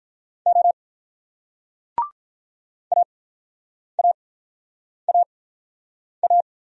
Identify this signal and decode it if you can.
Morse code, heard as a keyed tone, about 44 wpm, spaced as for 9 wpm, 700 Hz. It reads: O AAAA